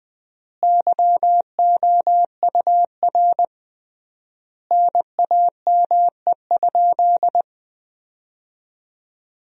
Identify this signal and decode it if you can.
Morse code, heard as a keyed tone, about 20 wpm, 700 Hz. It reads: YOUR NAME?
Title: 3x Word-Spacing - Sentences from Top 200 Words - 20wpm